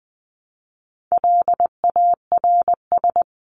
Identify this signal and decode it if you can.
Morse code, heard as a keyed tone, about 20 wpm, 700 Hz. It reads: LARS
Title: CWT Exchange 20wpm